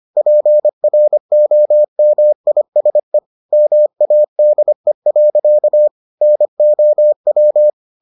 Morse code, heard as a keyed tone, about 25 wpm, 600 Hz. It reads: PROMISE MADE. NOW